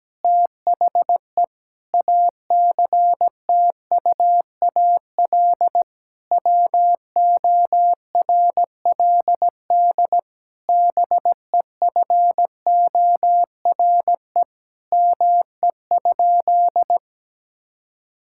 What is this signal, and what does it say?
Morse code, heard as a keyed tone, about 17 wpm, 700 Hz. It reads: THE ACTUAL WORLD BEFORE ME?
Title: Sentences from Top 1000 Words - Review Part 5 - 17wpm